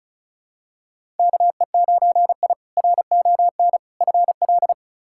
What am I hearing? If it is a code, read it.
Morse code, heard as a keyed tone, about 36 wpm, 700 Hz. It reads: KE9I RON FL
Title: K1USN SST QSO Practice - 35wpm